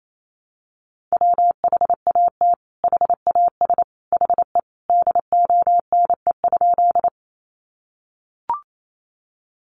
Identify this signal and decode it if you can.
Morse code, heard as a keyed tone, about 28 wpm, 700 Hz. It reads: WHAT HAS HE DONE?
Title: Sentences from Top 500 Words 28wpm